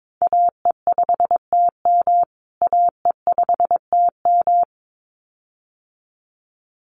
Morse code, heard as a keyed tone, about 22 wpm, 700 Hz. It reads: AE5TM AE5TM
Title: USA Call Signs Repeated 2x 22wpm